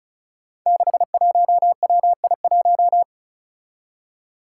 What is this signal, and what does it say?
Morse code, heard as a keyed tone, about 36 wpm, 700 Hz. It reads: 61WI1
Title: Five Challenging Letter-Number Combinations 35wpm